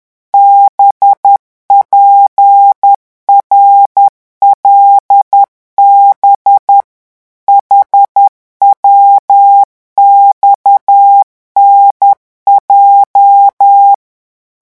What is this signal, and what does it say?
Morse code, heard as a keyed tone, about 10 wpm, 800 Hz. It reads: BPRLB HWXNJ